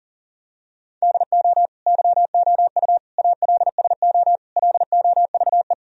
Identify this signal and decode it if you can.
Morse code, heard as a keyed tone, about 40 wpm, 700 Hz. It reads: DO YOU ALSO LOVE
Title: Sentences from Top 1000 Words - Review Part 1 - 40wpm